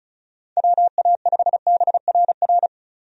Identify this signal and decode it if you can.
Morse code, heard as a keyed tone, about 36 wpm, 700 Hz. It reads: WA5BRR
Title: USA Call Signs - 6 Characters - 35wpm